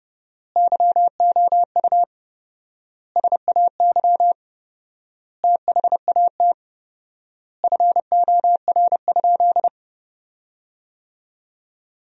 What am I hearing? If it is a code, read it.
Morse code, heard as a keyed tone, about 30 wpm, 700 Hz. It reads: YOU SAY THAT FOR?
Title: 4x Word-Spacing - Sentences from Top 200 Words - 30wpm